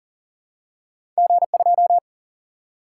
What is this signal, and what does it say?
Morse code, heard as a keyed tone, about 42 wpm, 700 Hz. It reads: G2
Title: Single-Letter Single-Number 40wpm